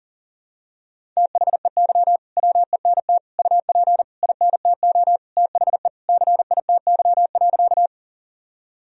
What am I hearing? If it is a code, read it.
Morse code, heard as a keyed tone, about 37 wpm, 700 Hz. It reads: THEY WENT UP INTO THE CITY.
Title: Sentences from Top 1000 Words - Review Part 1 - 40wpm